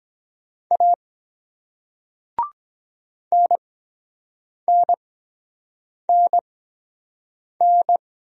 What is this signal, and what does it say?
Morse code, heard as a keyed tone, about 23 wpm, spaced as for 7 wpm, 700 Hz. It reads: A NNNN